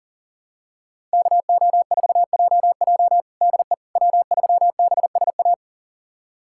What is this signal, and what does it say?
Morse code, heard as a keyed tone, about 37 wpm, 700 Hz. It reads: KO4JJ DE W3BSA
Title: USA Callsign DE Callsign - 5 Characters - 40wpm